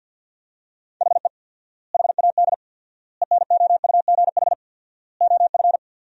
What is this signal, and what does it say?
Morse code, heard as a keyed tone, about 50 wpm, 700 Hz. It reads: HE HAD ENOUGH OF
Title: 4x Word-Spacing - Sentences from Top 400 Words - 50wpm